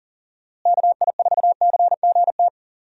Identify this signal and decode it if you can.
Morse code, heard as a keyed tone, about 40 wpm, 700 Hz. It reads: KI4CGT
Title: USA Call Signs - 6 Characters - Repeated 2x - 40wpm